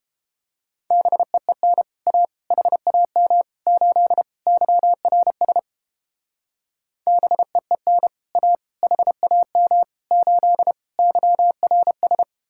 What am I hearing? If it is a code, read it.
Morse code, heard as a keyed tone, about 31 wpm, 700 Hz. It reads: BEEN A HAM 8 YRS BEEN A HAM 8 YRS